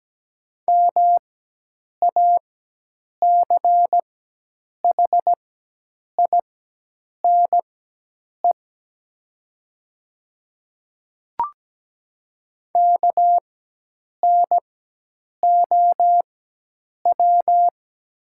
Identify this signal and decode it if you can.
Morse code, heard as a keyed tone, about 17 wpm, spaced as for 8 wpm, 700 Hz. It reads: MACHINE KNOW
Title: 4x Character Spacing - Top 500 Words - 17wpm